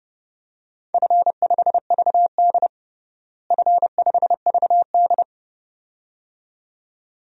Morse code, heard as a keyed tone, about 30 wpm, 700 Hz. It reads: F5VD F5VD
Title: DXCC-100 Call Signs - 4 Characters - Repeated 2x - 30wpm